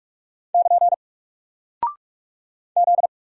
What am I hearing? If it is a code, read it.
Morse code, heard as a keyed tone, about 44 wpm, 700 Hz. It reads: <KN> Z
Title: Single Character (Letter:Number:Punctuation:Prosign) Alt 45wpm